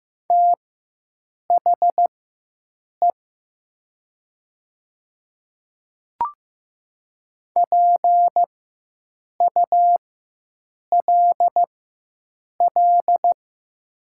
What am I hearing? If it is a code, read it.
Morse code, heard as a keyed tone, about 15 wpm, spaced as for 7 wpm, 700 Hz. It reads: THE PULL
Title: 4x Character Spacing - Top 500 Words - 15wpm